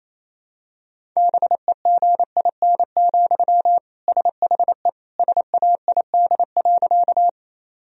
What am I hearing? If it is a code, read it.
Morse code, heard as a keyed tone, about 28 wpm, 700 Hz. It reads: BEGIN, SHE SAID.
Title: Sentences from Top 1000 Words - Review Part 1 - 28wpm